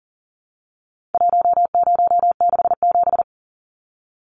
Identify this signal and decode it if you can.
Morse code, heard as a keyed tone, about 40 wpm, 700 Hz. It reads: 1067
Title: Numbers - 4-Digit 40wpm